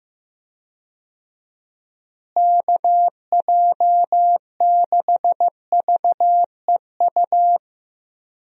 Morse code, heard as a keyed tone, about 15 wpm, 700 Hz. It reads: KJ6VEU
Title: USA Call Signs - 6 Characters - Repeated 3x - 15wpm